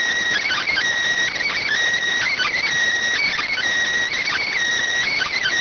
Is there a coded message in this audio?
Yes: Emergency Alert System (SAME) tones from 0.4 to 4.5 s, heard as harsh buzzing bursts.